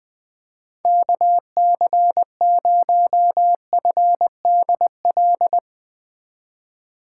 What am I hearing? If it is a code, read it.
Morse code, heard as a keyed tone, about 20 wpm, 700 Hz. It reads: KC0FDL
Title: USA Call Signs - 6 Characters - 3x Character Spacing - 20wpm